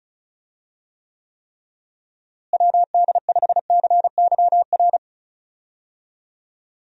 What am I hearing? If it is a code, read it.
Morse code, heard as a keyed tone, about 36 wpm, 700 Hz. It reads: WD5CYR
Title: USA Call Signs - 6 Characters - Repeated 3x - 35wpm